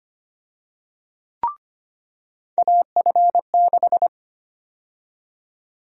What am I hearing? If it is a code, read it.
Morse code, heard as a keyed tone, about 25 wpm, 700 Hz. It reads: AF6
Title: USA Call Sign Prefixes - 25wpm